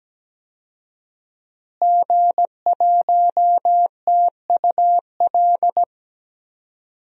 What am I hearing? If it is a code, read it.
Morse code, heard as a keyed tone, about 17 wpm, 700 Hz. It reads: G1TUL
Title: DXCC-100 Call Signs - 5 Characters - Repeated 3x - 17wpm